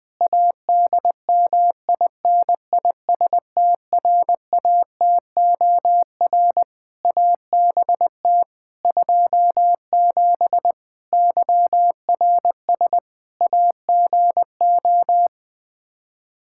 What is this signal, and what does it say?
Morse code, heard as a keyed tone, about 21 wpm, 700 Hz. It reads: ADMINISTRATOR ABT 27 YRS AGO
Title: K1USN SST QSO Practice 20wpm